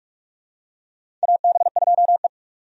Morse code, heard as a keyed tone, about 45 wpm, 700 Hz. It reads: AB2E
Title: USA Call Signs - 4 Characters - 45wpm